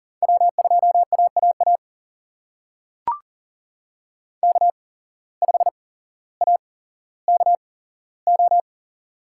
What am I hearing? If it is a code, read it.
Morse code, heard as a keyed tone, about 40 wpm, spaced as for 17 wpm, 700 Hz. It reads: W2AAA K5AKO
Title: USA Call Signs - 5 Characters - 8x Character Spacing - 40wpm